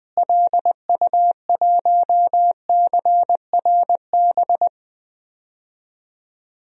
Morse code, heard as a keyed tone, about 20 wpm, 700 Hz. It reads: LU1CRB
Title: DXCC-100 Call Signs - 6 Characters - Repeated 2x - 20wpm